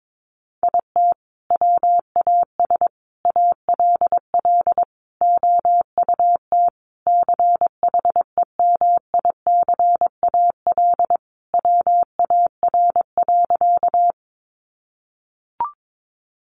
Morse code, heard as a keyed tone, about 22 wpm, 700 Hz. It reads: IT WAS ALL OUT CHEMICAL WAR.
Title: Sentences from Top 1200 Words 22wpm